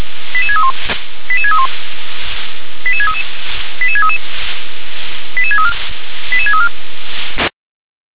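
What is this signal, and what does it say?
Five-tone selective call (ZVEI), heard as quick runs of five tones, tones of about 70 ms. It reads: 88631 88631 88633 88633 88634 88634
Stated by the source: ZVEI five-tone selective call 88730 received on an SDR.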